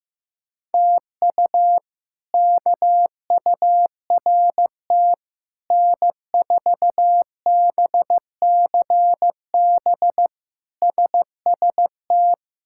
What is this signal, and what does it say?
Morse code, heard as a keyed tone, about 15 wpm, 700 Hz. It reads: TU KURT N4BCB SST
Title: K1USN SST QSO Practice - 15wpm